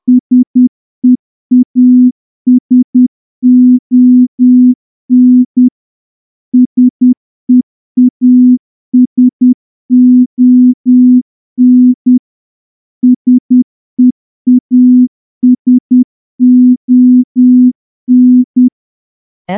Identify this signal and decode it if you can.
Morse code, heard as a keyed tone, about 10 wpm, 250 Hz. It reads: SEASON SEASON SEASON
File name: morse-training-10WPM